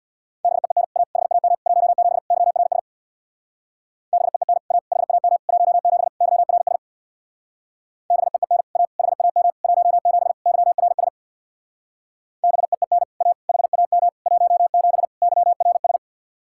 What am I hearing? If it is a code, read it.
Morse code, heard as a keyed tone, about 64 wpm, 700 Hz. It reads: BEEN A HAM 17 YRS BEEN A HAM 17 YRS BEEN A HAM 17 YRS BEEN A HAM 17 YRS